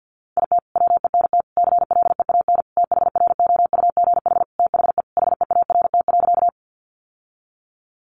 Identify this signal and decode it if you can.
Morse code, heard as a keyed tone, about 50 wpm, 700 Hz. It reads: IT WENT CLEAN THROUGH THE HEART.
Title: Sentences from Top 1200 Words 50wpm